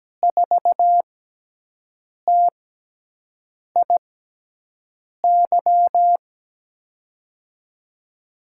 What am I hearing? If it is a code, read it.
Morse code, heard as a keyed tone, about 17 wpm, 700 Hz. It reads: 4 T I Y
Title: USA Call Signs - 6 Characters - 6x Character Spacing - 17wpm